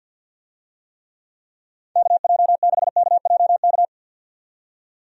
Morse code, heard as a keyed tone, about 51 wpm, 700 Hz. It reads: KJ6KJK
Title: USA Call Signs - 6 Characters - 50wpm